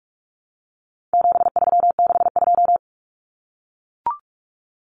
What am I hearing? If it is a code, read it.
Morse code, heard as a keyed tone, about 44 wpm, 700 Hz. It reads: 7362
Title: Numbers - 4-Digit 45wpm